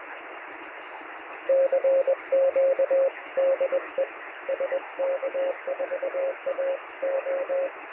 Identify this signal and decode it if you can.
Morse code, heard as a keyed tone, about 21 wpm, 540 Hz, with radio noise. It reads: CQ DE SK4AO
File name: sm7lcb / 1022z sk4ao 23cm